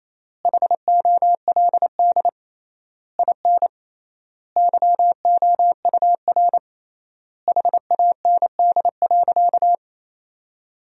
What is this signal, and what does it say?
Morse code, heard as a keyed tone, about 28 wpm, 700 Hz. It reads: HOLD IN YOUR HAND.
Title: 3x Word-Spacing - Sentences from Top 400 Words - 28wpm